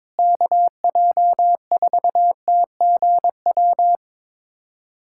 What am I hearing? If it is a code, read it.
Morse code, heard as a keyed tone, about 21 wpm, 700 Hz. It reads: KJ4TGW
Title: USA Callsign DE Callsign - 6 Characters - 22wpm